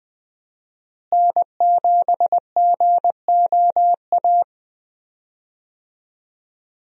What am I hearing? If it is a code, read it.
Morse code, heard as a keyed tone, about 20 wpm, 700 Hz. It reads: N7GOA
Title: USA Call Signs - 5 Characters - Repeated 2x - 20wpm